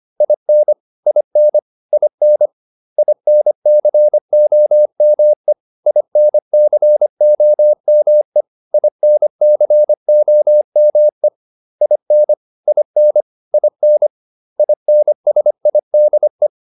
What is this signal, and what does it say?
Morse code, heard as a keyed tone, about 25 wpm, 600 Hz. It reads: IN IN IN INCOME INCOME INCOME IN IN IN INSIDE